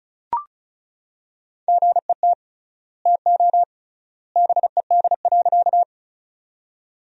Morse code, heard as a keyed tone, about 34 wpm, 700 Hz. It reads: GET TO BED.
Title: 3x Word-Spacing - Sentences from Top 500 Words - 35wpm